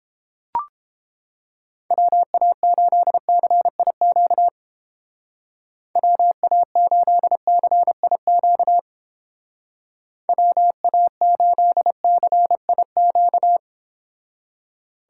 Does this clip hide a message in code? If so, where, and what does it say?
Yes: Morse code from 1.9 to 13.6 s, heard as a keyed tone, about 30 wpm, 700 Hz. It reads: WA8CIQ WA8CIQ WA8CIQ